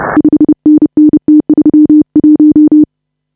Morse code, heard as a keyed tone, about 31 wpm, 300 Hz. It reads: 5NN T31
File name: AUDIO2019 / HA5BA CW